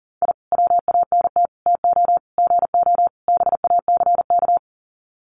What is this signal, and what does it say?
Morse code, heard as a keyed tone, about 40 wpm, 700 Hz. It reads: I WANT TO GO BACK
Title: Sets of 5 Words Encore 40wpm